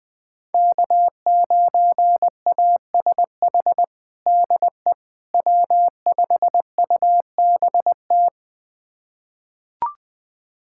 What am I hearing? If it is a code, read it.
Morse code, heard as a keyed tone, about 21 wpm, 700 Hz. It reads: K9ASH DE W5UBT
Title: USA Callsign DE Callsign - 5 Characters - 20wpm